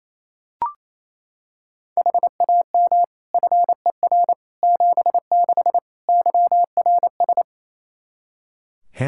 Morse code, heard as a keyed tone, about 27 wpm, 700 Hz. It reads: HAM FER 76 YRS
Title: K1USN SST QSO Practice 28wpm